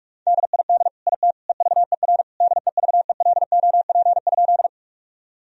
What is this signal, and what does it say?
Morse code, heard as a keyed tone, about 46 wpm, 700 Hz. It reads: DID IT EVER DEVELOP?
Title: Sentences from Top 1000 Words - Review Part 2 - 45wpm